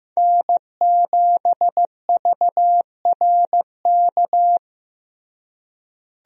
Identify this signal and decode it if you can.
Morse code, heard as a keyed tone, about 15 wpm, 700 Hz. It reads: N7VRK
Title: USA Call Signs - 5 Characters - Repeated 2x - 15wpm